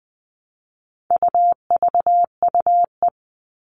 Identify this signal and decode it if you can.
Morse code, heard as a keyed tone, about 20 wpm, 700 Hz. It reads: UVUE
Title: All Cut Numbers - 4-Digit 20wpm